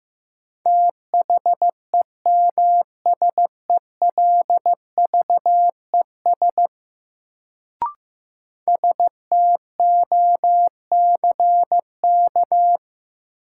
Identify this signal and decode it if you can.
Morse code, heard as a keyed tone, about 15 wpm, 700 Hz. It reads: THEMSELVES STOCK
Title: Sentences from Top 800 Words - New Word Drill - 15wpm